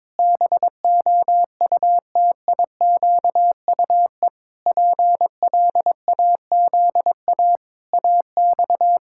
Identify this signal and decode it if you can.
Morse code, heard as a keyed tone, about 22 wpm, 700 Hz. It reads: BOUTIQUE PLAZA AX
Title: Pangram Challenge v2 - 22wpm